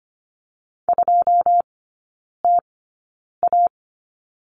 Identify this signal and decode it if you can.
Morse code, heard as a keyed tone, about 25 wpm, spaced as for 9 wpm, 700 Hz. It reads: 2TA
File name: course / 25:10wpm / 093 Lesson 16e - Letter c - Three Characters 25wpm